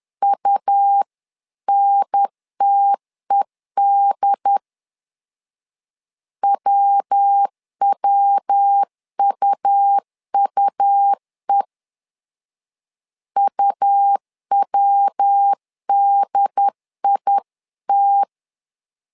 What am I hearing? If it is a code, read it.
Morse code, heard as a keyed tone, about 10 wpm, 800 Hz. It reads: UNTED WWUUE UWDIT
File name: LEZIONI / 3 MWRO / 3 MW 10wpm 1959